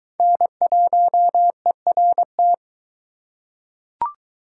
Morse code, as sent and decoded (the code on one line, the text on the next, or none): -. .---- . .-. -
N1ERT